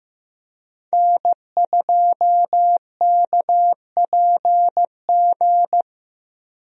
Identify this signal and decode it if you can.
Morse code, heard as a keyed tone, about 15 wpm, 700 Hz. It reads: N2KPG
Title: USA Call Signs - 5 Characters - Repeated 2x - 15wpm